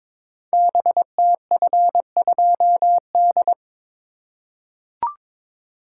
Morse code, heard as a keyed tone, about 22 wpm, 700 Hz. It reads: BTF2D
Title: Code Groups - 5 Letter:Numbers - 22wpm